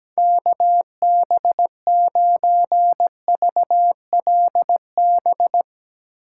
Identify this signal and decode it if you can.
Morse code, heard as a keyed tone, about 17 wpm, 700 Hz. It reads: KB9VLB
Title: USA Call Signs - 3x Character Spacing - 17wpm